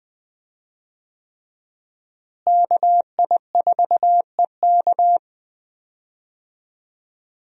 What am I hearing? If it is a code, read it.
Morse code, heard as a keyed tone, about 20 wpm, 700 Hz. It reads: KI4EK
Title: USA Call Signs - 5 Characters - Repeated 3x - 20wpm